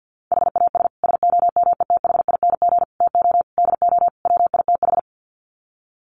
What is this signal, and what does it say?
Morse code, heard as a keyed tone, about 50 wpm, 700 Hz. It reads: HAS SOMETHING TO DO WITH
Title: Sets of 5 Words Encore 50wpm